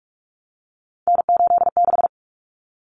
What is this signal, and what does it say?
Morse code, heard as a keyed tone, about 44 wpm, 700 Hz. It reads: N86
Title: Common Cut Numbers - 3-Digit 45wpm